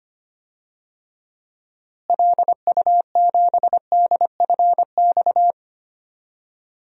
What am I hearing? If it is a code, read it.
Morse code, heard as a keyed tone, about 25 wpm, 700 Hz. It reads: LU7DFX
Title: DXCC-100 Call Signs - 6 Characters - Repeated 2x - 25wpm